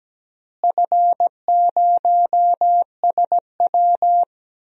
Morse code, heard as a keyed tone, about 17 wpm, 700 Hz. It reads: F0SW
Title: DXCC-100 Call Signs - 4 Characters - Repeated 3x - 17wpm